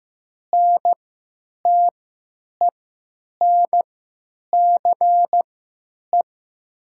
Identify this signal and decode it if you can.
Morse code, heard as a keyed tone, about 15 wpm, spaced as for 9 wpm, 700 Hz. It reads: NTENCE
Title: 3x Character Spacing - Top 300 Words - 15wpm